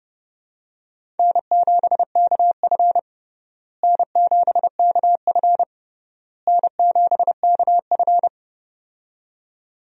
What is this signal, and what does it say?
Morse code, heard as a keyed tone, about 30 wpm, 700 Hz. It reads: N7KF N7KF N7KF
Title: USA Call Signs - 4 Characters - Repeated 3x - 30wpm